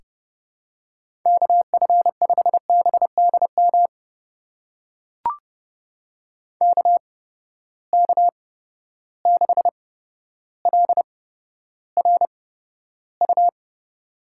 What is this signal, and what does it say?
Morse code, heard as a keyed tone, about 30 wpm, spaced as for 13 wpm, 700 Hz. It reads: KF5BDM KK6LRU
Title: USA Call Signs - 6 Characters - 8x Character Spacing - 30wpm